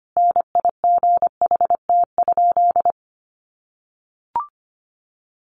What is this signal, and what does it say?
Morse code, heard as a keyed tone, about 25 wpm, 700 Hz. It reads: NIGHT?
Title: Sentences from Top 600 Words 25wpm